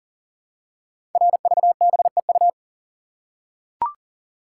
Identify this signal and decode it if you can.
Morse code, heard as a keyed tone, about 38 wpm, 690 Hz. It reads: RVBEU
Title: Code Groups - 5 Letters - 40wpm